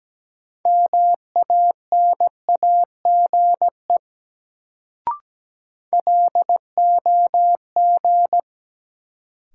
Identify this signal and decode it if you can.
Morse code, heard as a keyed tone, about 17 wpm, 700 Hz. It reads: MANAGE LOG
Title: Sentences from Top 1400 Words - New Word Drill Alt - 17wpm